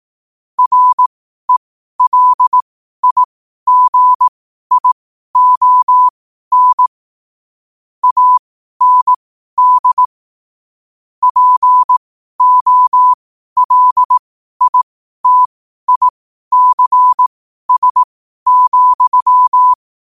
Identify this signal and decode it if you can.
Morse code, heard as a keyed tone, about 18 wpm, spaced as for 12 wpm, 1000 Hz. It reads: RELIGION AND POLITICS,
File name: qotdfiles / 12wpm